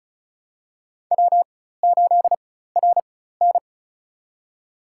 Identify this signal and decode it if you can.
Morse code, heard as a keyed tone, about 36 wpm, spaced as for 16 wpm, 700 Hz. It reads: W8RN
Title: USA Call Signs - 4 Characters - 4x Character Spacing - 35wpm